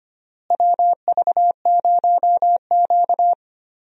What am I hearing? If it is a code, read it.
Morse code, heard as a keyed tone, about 25 wpm, 700 Hz. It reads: WV0Q
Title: USA Call Signs - 4 Characters - Repeated 3x - 25wpm